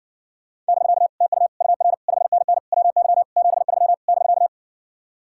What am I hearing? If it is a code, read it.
Morse code, heard as a keyed tone, about 60 wpm, 700 Hz. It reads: <BK> TU UR 5NN WY 73 <BK>